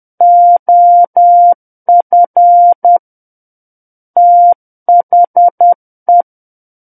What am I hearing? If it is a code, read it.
Morse code, heard as a keyed tone, about 10 wpm, 700 Hz. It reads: OF THE
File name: Book One / 01-Chapter 1 - 10 WPM